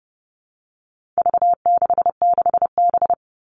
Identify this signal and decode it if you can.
Morse code, heard as a keyed tone, about 30 wpm, 700 Hz. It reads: V66B
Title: All Cut Numbers - 4-Digit 30wpm